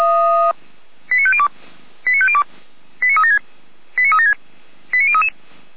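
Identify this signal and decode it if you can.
Five-tone selective call (ZVEI), heard as quick runs of five tones, tones of about 70 ms. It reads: 89502 89502 89376 89376 89033
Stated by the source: Five-tone ZVEI selective call received by SDR.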